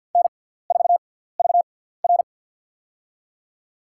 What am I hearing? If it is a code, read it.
Morse code, heard as a keyed tone, about 53 wpm, spaced as for 17 wpm, 700 Hz. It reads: N4VR
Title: USA Call Signs - 6x Character Spacing - 50wpm